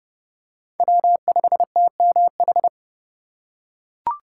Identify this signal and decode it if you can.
Morse code, heard as a keyed tone, about 29 wpm, 700 Hz. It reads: W5TMH